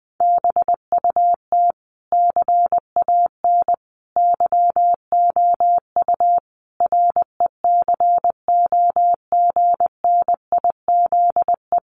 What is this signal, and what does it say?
Morse code, heard as a keyed tone, about 20 wpm, 700 Hz. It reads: BUT CAN YOU RECOGNIZE
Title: Sentences from Top 1400 Words 20wpm